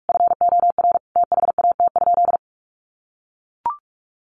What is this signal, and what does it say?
Morse code, heard as a keyed tone, about 44 wpm, 700 Hz. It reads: FOR THAT?
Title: Sentences from Top 1200 Words 45wpm